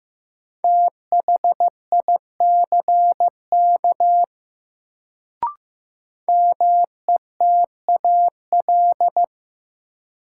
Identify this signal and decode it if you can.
Morse code, heard as a keyed tone, about 15 wpm, 700 Hz. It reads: THICK METAL
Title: Sentences from Top 1600 Words - New Word Drill Alt - 15wpm